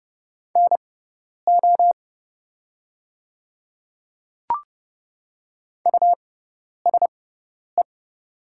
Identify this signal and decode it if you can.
Morse code, heard as a keyed tone, about 30 wpm, spaced as for 10 wpm, 700 Hz. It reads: NO USE
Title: 6x Character Spacing - Top 100 Words - 30wpm